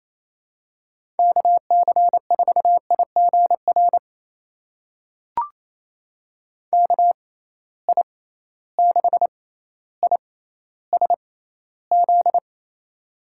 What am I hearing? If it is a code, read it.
Morse code, heard as a keyed tone, about 28 wpm, spaced as for 14 wpm, 700 Hz. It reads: KC4IGR KI6ISZ
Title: USA Call Signs - 6 Characters - 6x Character Spacing - 28wpm